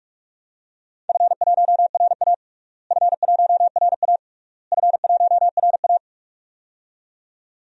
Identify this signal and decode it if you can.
Morse code, heard as a keyed tone, about 41 wpm, 700 Hz. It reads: F1RA F1RA F1RA
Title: DXCC-100 Call Signs - 4 Characters - Repeated 3x - 45wpm